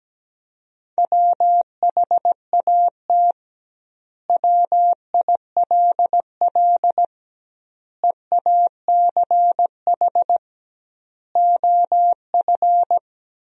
Morse code, heard as a keyed tone, about 17 wpm, 700 Hz. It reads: WHAT WILL EACH OF
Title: 2x Word-Spacing - Sentences from Top 200 Words - 17wpm